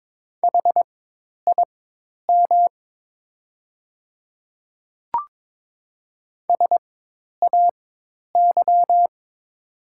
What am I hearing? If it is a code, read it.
Morse code, heard as a keyed tone, about 22 wpm, spaced as for 10 wpm, 700 Hz. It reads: HIM SAY
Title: 4x Character Spacing - Top 200 Words - 22wpm